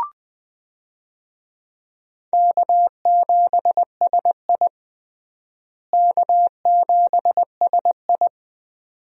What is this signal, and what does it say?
Morse code, heard as a keyed tone, about 20 wpm, 700 Hz. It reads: K7SI K7SI
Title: USA Call Signs - 4 Characters - Repeated 2x - 20wpm